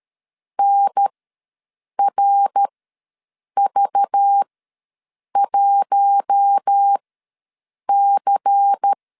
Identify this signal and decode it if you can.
Morse code, heard as a keyed tone, about 13 wpm, spaced as for 7 wpm, 800 Hz. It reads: NRV1C
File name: LEZIONI / 6 PKX0 / 6 PKX0 ws 12wpm 3448